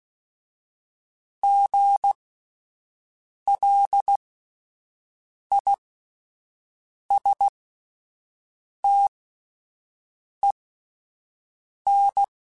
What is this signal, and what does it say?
Morse code, heard as a keyed tone, about 16 wpm, spaced as for 5 wpm, 800 Hz. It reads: GLISTEN
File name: Spaced Alphas / Spaced Alphas mp3s / Gs sp @10